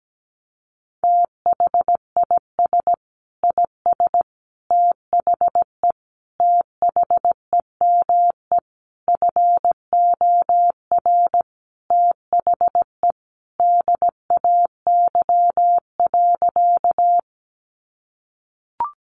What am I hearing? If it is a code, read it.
Morse code, heard as a keyed tone, about 17 wpm, 700 Hz. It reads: THIS IS THE THEME FOR THE DAY.
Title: Sentences from Top 1100 Words 17wpm